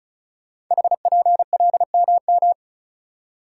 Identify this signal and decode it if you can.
Morse code, heard as a keyed tone, about 36 wpm, 690 Hz. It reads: HPLMM
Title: Code Groups - 5 Letters - 35wpm